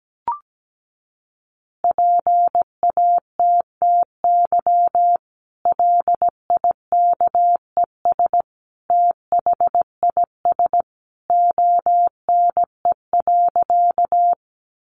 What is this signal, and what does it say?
Morse code, heard as a keyed tone, about 17 wpm, 700 Hz. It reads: PATTY LIKES THIS ONE.
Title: Additional Easy Phrases 17wpm